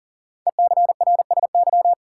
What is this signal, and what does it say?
Morse code, heard as a keyed tone, about 40 wpm, 700 Hz. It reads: ECRSY
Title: Code Groups - 5 Letters - 40wpm